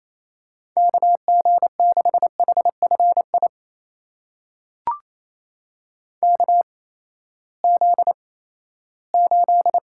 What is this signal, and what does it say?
Morse code, heard as a keyed tone, about 28 wpm, spaced as for 16 wpm, 700 Hz. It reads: KG6HFI KZ8
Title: USA Call Signs - 8x Character Spacing - 28wpm